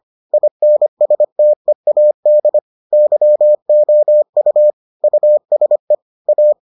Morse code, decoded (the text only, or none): INSTEAD YOU USE A